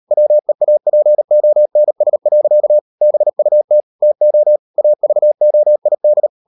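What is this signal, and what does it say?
Morse code, heard as a keyed tone, about 40 wpm, 600 Hz. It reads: WEAPONS. BUT TO AVOID
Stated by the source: Morse code practice at 35–39 wpm sending English text of Ken Follett's novel Never, from CW med Gnister.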